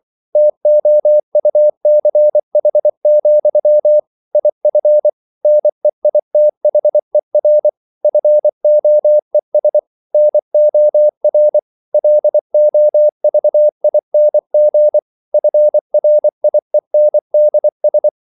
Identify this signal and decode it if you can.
Morse code, heard as a keyed tone, about 24 wpm, 600 Hz. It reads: TOUCH, IF NEITHER FOES NOR LOVING FRIENDS